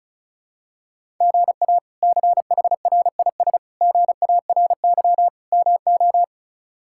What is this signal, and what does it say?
Morse code, heard as a keyed tone, about 36 wpm, 700 Hz. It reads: GA CHRIS GARY MO
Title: K1USN SST QSO Practice - 35wpm